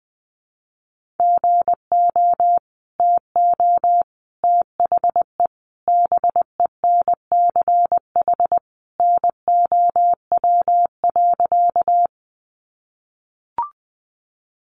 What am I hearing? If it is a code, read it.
Morse code, heard as a keyed tone, about 19 wpm, 700 Hz. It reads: GO TO THE BENCH NOW.
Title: Sentences from Top 1400 Words 20wpm